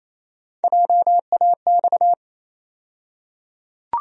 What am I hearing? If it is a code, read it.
Morse code, heard as a keyed tone, about 28 wpm, 700 Hz. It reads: JAX
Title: USA Call Sign Postfixes - 3 Letters 28wpm